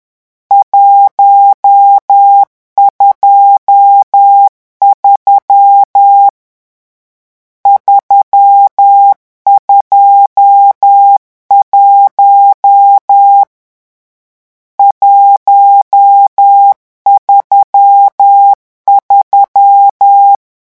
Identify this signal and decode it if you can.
Morse code, heard as a keyed tone, about 10 wpm, 800 Hz. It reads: 123 321 133
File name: morse mp3 / lecon10-vitesse 10